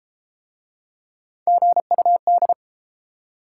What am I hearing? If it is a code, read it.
Morse code, heard as a keyed tone, about 31 wpm, 700 Hz. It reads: GUD